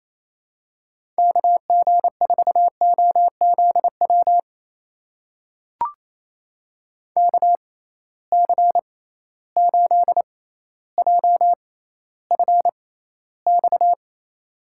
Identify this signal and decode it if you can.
Morse code, heard as a keyed tone, about 28 wpm, spaced as for 14 wpm, 700 Hz. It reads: KG4OZW KC8JFX
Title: USA Call Signs - 6 Characters - 6x Character Spacing - 28wpm